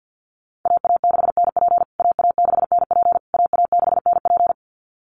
Morse code, heard as a keyed tone, about 50 wpm, 700 Hz. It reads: AA6NP AA6NP AA6NP
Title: USA Call Signs Repeated 3x 50wpm